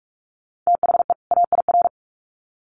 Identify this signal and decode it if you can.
Morse code, heard as a keyed tone, about 44 wpm, 700 Hz. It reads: THE AIR
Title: Sets of 2 Words 45wpm